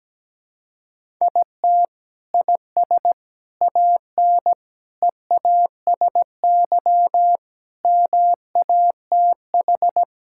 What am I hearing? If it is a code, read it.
Morse code, heard as a keyed tone, about 17 wpm, 700 Hz. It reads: IT IS AN EASY MATH
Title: Sentences from Top 1000 Words - Review Part 6 - 17wpm